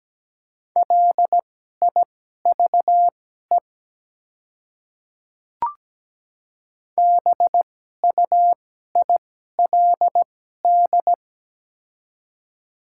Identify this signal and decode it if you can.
Morse code, heard as a keyed tone, about 17 wpm, spaced as for 12 wpm, 700 Hz. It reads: LIVE BUILD
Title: 2x Character Spacing - Top 300 Words - 17wpm